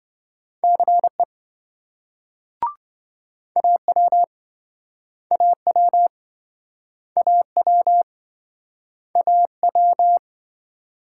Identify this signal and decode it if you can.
Morse code, heard as a keyed tone, about 27 wpm, 700 Hz. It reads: CE AW AW AW AW